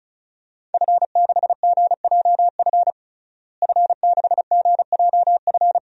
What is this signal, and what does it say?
Morse code, heard as a keyed tone, about 34 wpm, 700 Hz. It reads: F6GJF F6GJF
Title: DXCC-100 Call Signs - 5 Characters - Repeated 3x - 35wpm